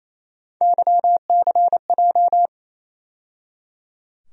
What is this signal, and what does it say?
Morse code, heard as a keyed tone, about 27 wpm, 700 Hz. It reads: YCJ
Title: USA Call Sign Postfixes - 3 Letters 28wpm